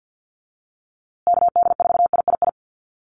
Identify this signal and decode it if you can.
Morse code, heard as a keyed tone, about 50 wpm, 700 Hz. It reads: KD4III
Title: USA Call Signs Repeated 2x 50wpm